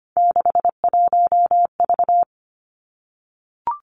Morse code, heard as a keyed tone, about 25 wpm, 700 Hz. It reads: 61V
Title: Three Challenging Letter-Number Combinations 25wpm